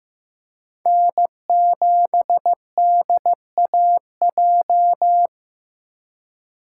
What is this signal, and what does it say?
Morse code, heard as a keyed tone, about 18 wpm, 700 Hz. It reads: N7DAJ
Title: Speed Racing - USA Call Signs 15wpm